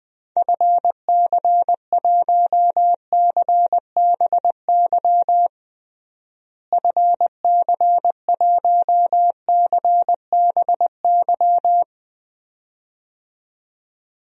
Morse code, heard as a keyed tone, about 20 wpm, 700 Hz. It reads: FC1CBY FC1CBY
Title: DXCC-100 Call Signs - 6 Characters - Repeated 2x - 20wpm